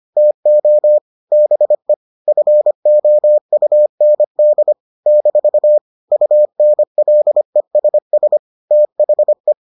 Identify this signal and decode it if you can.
Morse code, heard as a keyed tone, about 25 wpm, 600 Hz. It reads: TO BE FOUND - UNLESS THE